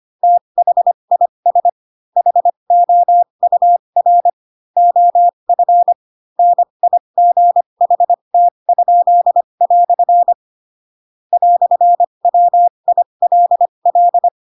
Morse code, heard as a keyed tone, about 25 wpm, spaced as for 22 wpm, 700 Hz. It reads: THIS HOUR OF NIGHT?" "WILL
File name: Pinocchio / Pinocchio ch6 18 25 ik1hge